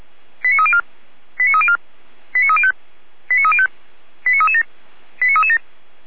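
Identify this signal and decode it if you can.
Five-tone selective call (ZVEI), heard as quick runs of five tones, tones of about 70 ms. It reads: 89304 89304 89305 89305 89337 89337
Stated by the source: Five-tone ZVEI selective call received on an SDR.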